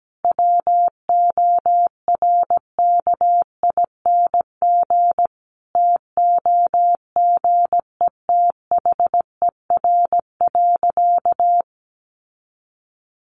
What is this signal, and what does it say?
Morse code, heard as a keyed tone, about 17 wpm, 700 Hz. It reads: WORKING TOGETHER.
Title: Sentences from Top 1200 Words 17wpm